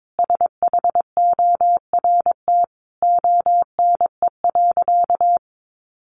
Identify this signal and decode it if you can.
Morse code, heard as a keyed tone, about 22 wpm, 700 Hz. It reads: SHORT ONE.
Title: Sentences from Top 1100 Words 22wpm